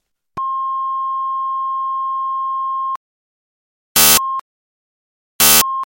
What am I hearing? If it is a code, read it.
There is no code.